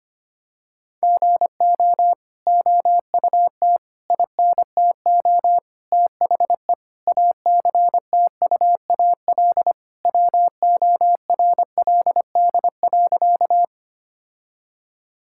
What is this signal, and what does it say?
Morse code, heard as a keyed tone, about 25 wpm, 700 Hz. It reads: GO OUT INTO THE ACTUAL WORLD.
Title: Sentences from Top 1000 Words - Review Part 5 - 25wpm